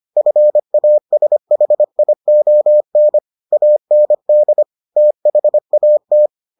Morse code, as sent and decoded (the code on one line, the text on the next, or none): ..-. .- ... .... .. --- -. / .- -. -.. / - .... .- -
FASHION AND THAT